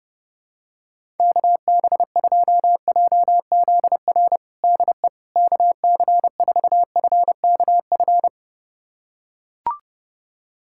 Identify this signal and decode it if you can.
Morse code, heard as a keyed tone, about 32 wpm, 700 Hz. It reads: KB2JZR DE KC4FKF
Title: USA Callsign DE Callsign - 6 Characters - 30wpm